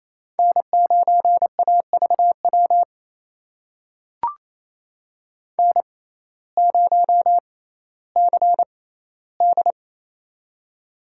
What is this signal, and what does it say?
Morse code, heard as a keyed tone, about 28 wpm, spaced as for 15 wpm, 700 Hz. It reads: N9AVW N0CD
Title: USA Call Signs - 6x Character Spacing - 28wpm